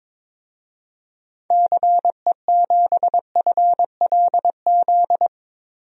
Morse code, heard as a keyed tone, about 22 wpm, 700 Hz. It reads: CE7FLZ
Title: DXCC-100 Call Signs - 6 Characters - 22wpm